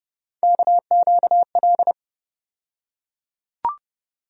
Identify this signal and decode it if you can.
Morse code, heard as a keyed tone, about 30 wpm, 700 Hz. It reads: KQL
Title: USA Call Sign Postfixes - 3 Letters 30wpm